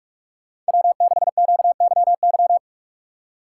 Morse code, heard as a keyed tone, about 47 wpm, 690 Hz. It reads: W6QYY
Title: USA Call Signs - 5 Characters - 2x Character Spacing - 45wpm